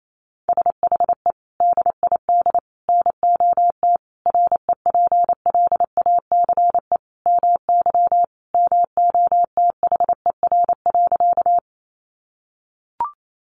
Morse code, heard as a keyed tone, about 28 wpm, 700 Hz. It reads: SHE DID NOT REPLACE MY MOTHER.
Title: Sentences from Top 1400 Words 28wpm